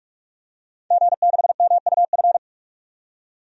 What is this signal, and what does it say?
Morse code, heard as a keyed tone, about 46 wpm, 700 Hz. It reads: G6MUF
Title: DXCC-100 Call Signs - 5 Characters - 45wpm